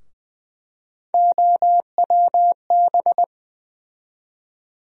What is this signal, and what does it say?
Morse code, heard as a keyed tone, about 20 wpm, 700 Hz. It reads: OWB
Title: USA Call Sign Postfixes - 3 Letters 20wpm